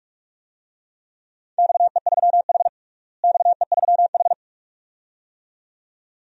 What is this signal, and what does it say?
Morse code, heard as a keyed tone, about 46 wpm, 700 Hz. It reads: XE3H XE3H
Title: DXCC-100 Call Signs - 4 Characters - Repeated 2x - 45wpm